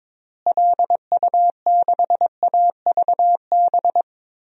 Morse code, heard as a keyed tone, about 22 wpm, 700 Hz. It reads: LU6AVB
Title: DXCC-100 Call Signs - 6 Characters - Repeated 3x - 22wpm